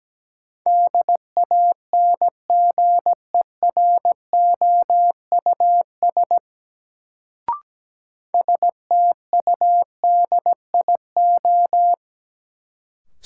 Morse code, heard as a keyed tone, about 17 wpm, 700 Hz. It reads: DANGEROUS STUDIO
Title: Sentences from Top 1200 Words - New Word Drill - 17wpm